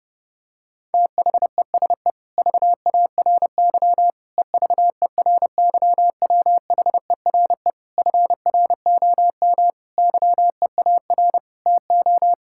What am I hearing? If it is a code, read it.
Morse code, heard as a keyed tone, about 32 wpm, 700 Hz. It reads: THESE VARY EVERYWHERE FROM YEAR TO
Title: Sentences from Top 1600 Words 30wpm